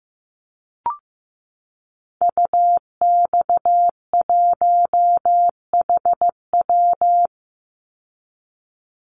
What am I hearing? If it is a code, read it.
Morse code, heard as a keyed tone, about 15 wpm, 700 Hz. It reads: UX1HW
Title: Call Signs Repeated 3x - 15wpm